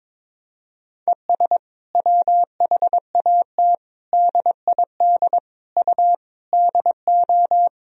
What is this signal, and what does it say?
Morse code, heard as a keyed tone, about 22 wpm, 700 Hz. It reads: ES WHAT DID U DO
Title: Sentences from Top 300 Words and Abbreviations 22wpm